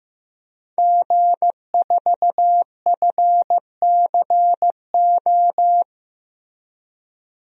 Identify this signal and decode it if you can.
Morse code, heard as a keyed tone, about 15 wpm, 700 Hz. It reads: G4FCO
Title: DXCC-100 Call Signs - 5 Characters - Repeated 3x - 15wpm